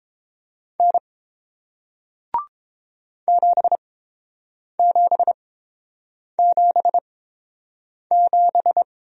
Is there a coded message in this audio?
Yes: Morse code from 0.8 to 8.8 s, heard as a keyed tone, about 30 wpm, spaced as for 7 wpm, 700 Hz. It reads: N 7777